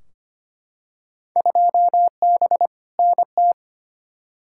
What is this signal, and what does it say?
Morse code, heard as a keyed tone, about 25 wpm, 700 Hz. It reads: 2B NT
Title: Field Day Exchanges Encore - 25wpm